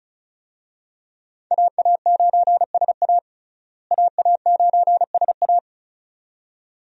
Morse code, heard as a keyed tone, about 34 wpm, 700 Hz. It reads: AA9SA AA9SA
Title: USA Call Signs - 5 Characters - Repeated 2x - 35wpm